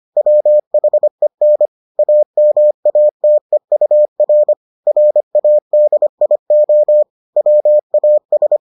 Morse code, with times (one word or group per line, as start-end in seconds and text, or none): 0.2-1.7 WHEN
2.0-4.5 AMATEUR
4.9-7.0 RADIO
7.4-8.6 WAS